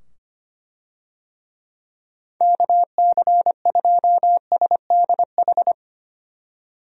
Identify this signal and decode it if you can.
Morse code, heard as a keyed tone, about 25 wpm, 700 Hz. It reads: KC2SDH
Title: USA Call Signs - 6 Characters - 25wpm